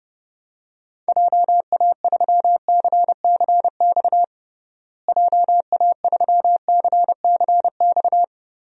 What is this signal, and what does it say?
Morse code, heard as a keyed tone, about 30 wpm, 700 Hz. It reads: JA3CCX JA3CCX
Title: DXCC-100 Call Signs - 6 Characters - Repeated 3x - 30wpm